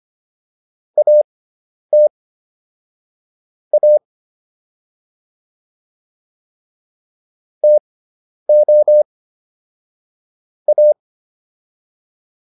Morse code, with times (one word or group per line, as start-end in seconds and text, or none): 1.0-2.1 AT
3.7-4.0 A
7.6-9.0 TO
10.7-10.9 A